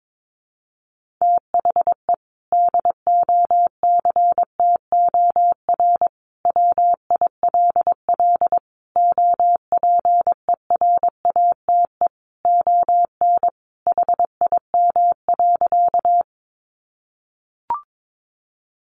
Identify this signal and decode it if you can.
Morse code, heard as a keyed tone, about 22 wpm, 700 Hz. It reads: THE DOCTOR WILL OPERATE ON HIM.
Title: Sentences from Top 1500 Words 22wpm